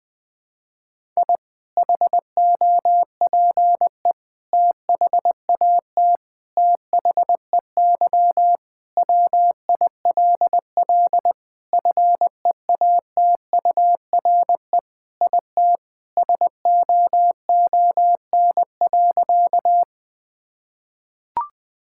Morse code, heard as a keyed tone, about 19 wpm, 700 Hz. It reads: I HOPE THAT THEY WILL FEATURE IT SOON.
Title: Sentences from Top 1000 Words - Review Part 5 - 20wpm